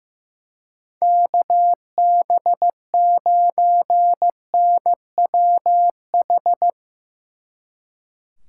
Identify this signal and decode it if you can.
Morse code, heard as a keyed tone, about 15 wpm, 700 Hz. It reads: KB9NWH
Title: Speed Racing - USA 2x3 Call Signs 15wpm